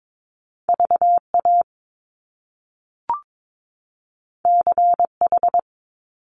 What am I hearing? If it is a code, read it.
Morse code, heard as a keyed tone, about 22 wpm, 700 Hz. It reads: VA CH
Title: Top Two Letter Combinations in Words 22wpm